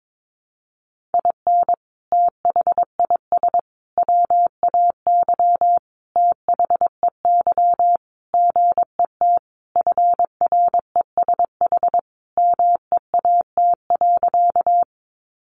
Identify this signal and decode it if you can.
Morse code, heard as a keyed tone, about 22 wpm, 700 Hz. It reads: IN THIS WAY THEY GET FRESH MEAT.
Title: Sentences from Top 1200 Words 22wpm